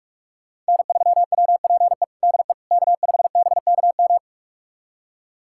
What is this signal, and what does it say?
Morse code, heard as a keyed tone, about 46 wpm, 700 Hz. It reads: N3WPE DE K5BKM
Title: USA Callsign DE Callsign - 5 Characters - 45wpm